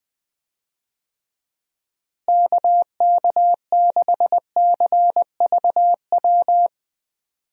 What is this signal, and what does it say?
Morse code, heard as a keyed tone, about 20 wpm, 700 Hz. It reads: KK6CVW
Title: USA Call Signs - 6 Characters - Repeated 3x - 20wpm